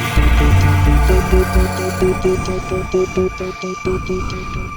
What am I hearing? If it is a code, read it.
There is no code.